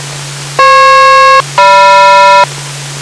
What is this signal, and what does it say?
Aircraft SELCAL, heard as two pairs of tones. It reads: FP-GM